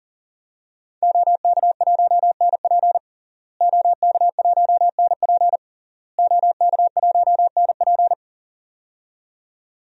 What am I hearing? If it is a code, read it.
Morse code, heard as a keyed tone, about 40 wpm, 700 Hz. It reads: OK1NP OK1NP OK1NP
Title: DXCC-100 Call Signs - 5 Characters - Repeated 3x - 40wpm